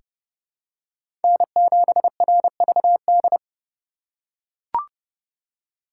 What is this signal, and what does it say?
Morse code, heard as a keyed tone, about 30 wpm, 700 Hz. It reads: N7RVD
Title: K1USN SST QSO Practice - 30wpm